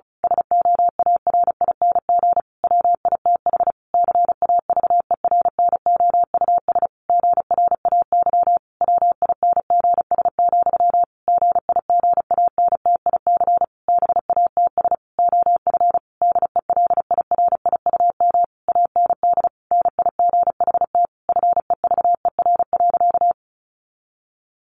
Morse code, heard as a keyed tone, about 34 wpm, 700 Hz. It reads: SOARING WITH CAVERNOUS GRAY WINGS, GIGANTIC BATS OF DELIRIUM AND NIGHT FEVER.
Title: A Sound Of Thunder 35wpm